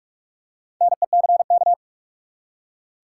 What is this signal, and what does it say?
Morse code, heard as a keyed tone, about 47 wpm, 700 Hz. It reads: NECK
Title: Four Letter Words - 45wpm